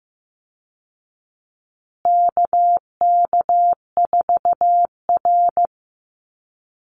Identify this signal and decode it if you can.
Morse code, heard as a keyed tone, about 15 wpm, 700 Hz. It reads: KK4R
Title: Call Signs Repeated 2x - 15wpm